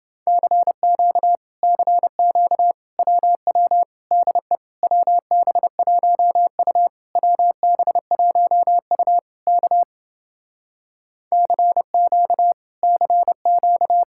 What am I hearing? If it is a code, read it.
Morse code, heard as a keyed tone, about 29 wpm, 700 Hz. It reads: CQ CQ WW DE WB1U WB1U K CQ CQ